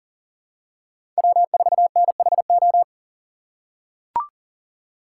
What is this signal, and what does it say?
Morse code, heard as a keyed tone, about 40 wpm, 700 Hz. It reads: W4NHO
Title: USA Call Signs - 5 Characters - 3x Character Spacing - 40wpm